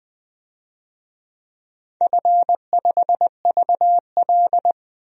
Morse code, heard as a keyed tone, about 20 wpm, 700 Hz. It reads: F5VL
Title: DXCC-100 Call Signs - 4 Characters - Repeated 3x - 20wpm